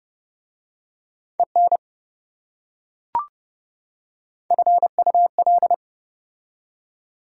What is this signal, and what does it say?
Morse code, heard as a keyed tone, about 29 wpm, 700 Hz. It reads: EN FUL